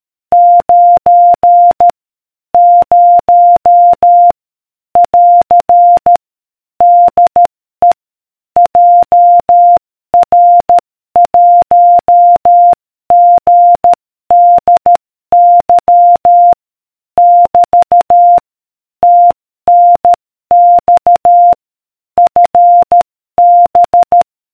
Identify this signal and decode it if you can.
Morse code, heard as a keyed tone, about 13 wpm, 700 Hz. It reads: 9 0 <AR> DE JR1GDY <BT> TNX FB